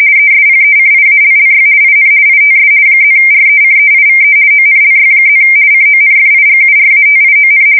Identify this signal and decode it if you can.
RTTY, about 50 baud. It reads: ESPECIALLY IN ITS +RECEIVE ONLY+ CONFIGURATION
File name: fsk170-50B / teletype-corp